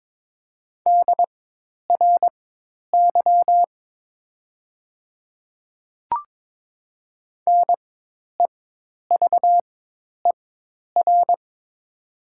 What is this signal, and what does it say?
Morse code, heard as a keyed tone, about 22 wpm, spaced as for 10 wpm, 700 Hz. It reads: DRY NEVER
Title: 4x Character Spacing - Top 500 Words - 22wpm